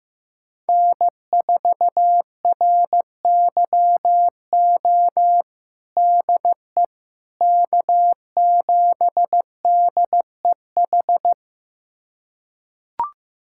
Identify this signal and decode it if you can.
Morse code, heard as a keyed tone, about 16 wpm, 700 Hz. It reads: N4RYO DE K7DEH
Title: USA Callsign DE Callsign - 5 Characters - 15wpm